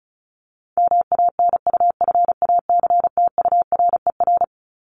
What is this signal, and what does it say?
Morse code, heard as a keyed tone, about 34 wpm, 700 Hz. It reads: MANUFACTURER
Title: Top 1751-2000 Words 35wpm